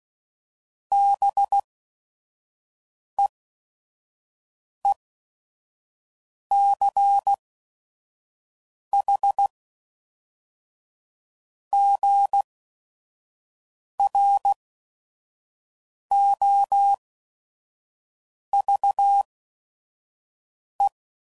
Morse code, heard as a keyed tone, about 16 wpm, spaced as for 5 wpm, 800 Hz. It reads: BEECHGROVE
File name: CW Code Groups @5WPM / fromSM's / Cities